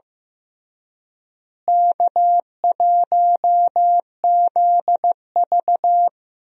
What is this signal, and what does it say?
Morse code, heard as a keyed tone, about 15 wpm, 700 Hz. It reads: K1ZV
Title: USA Call Signs - 4 Characters - Repeated 2x - 15wpm